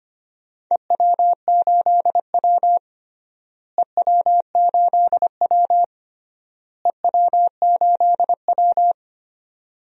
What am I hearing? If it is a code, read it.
Morse code, heard as a keyed tone, about 25 wpm, 700 Hz. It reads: EW8W EW8W EW8W
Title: DXCC-100 Call Signs - 4 Characters - Repeated 3x - 25wpm